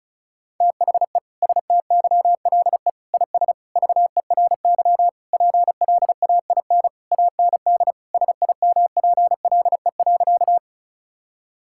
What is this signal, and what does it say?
Morse code, heard as a keyed tone, about 36 wpm, 700 Hz. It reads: THE STYLE IS VERY PLAIN AND SIMPLE.
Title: Sentences from Top 1000 Words - Review Part 5 - 35wpm